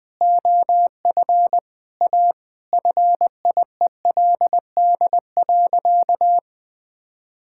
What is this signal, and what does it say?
Morse code, heard as a keyed tone, about 21 wpm, 700 Hz. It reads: OF A FIELD.
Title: Sentences from Top 1000 Words - Review Part 6 - 20wpm